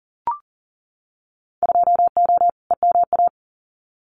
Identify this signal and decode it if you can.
Morse code, heard as a keyed tone, about 40 wpm, 700 Hz. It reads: 2O EMA
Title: Winter Field Day Exchanges 40wpm